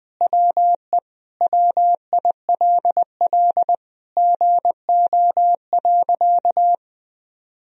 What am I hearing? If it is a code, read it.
Morse code, heard as a keyed tone, about 20 wpm, 700 Hz. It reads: WE WILL GO.
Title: Sentences from Top 1000 Words - Review Part 1 - 20wpm